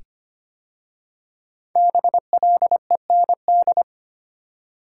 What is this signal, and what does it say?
Morse code, heard as a keyed tone, about 25 wpm, 700 Hz. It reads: BLEND
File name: 5 Letter Words - First 3 Letters Given - 25wpm